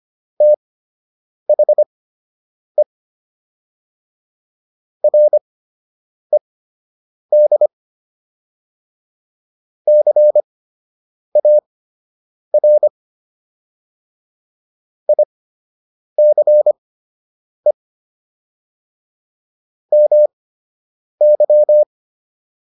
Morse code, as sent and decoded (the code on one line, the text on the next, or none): - .... / .-. -.. / -.-. .- .-. / .. -.-. / -- -.--
THE RED CAR ICE MY